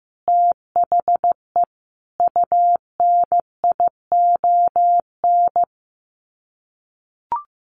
Morse code, heard as a keyed tone, about 14 wpm, 700 Hz. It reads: THE UNION
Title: Sets of 5 Words Encore 15wpm